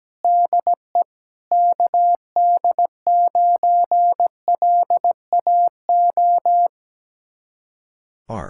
Morse code, heard as a keyed tone, about 17 wpm, 700 Hz. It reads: DE KD9LAO
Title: Speed-Racing Common QSO Phrases - 17wpm